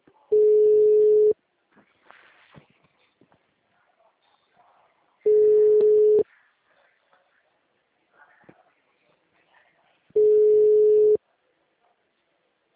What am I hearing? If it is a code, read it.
Ringback tone (Europe, 425 Hz).
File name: out- -2019- - - .10074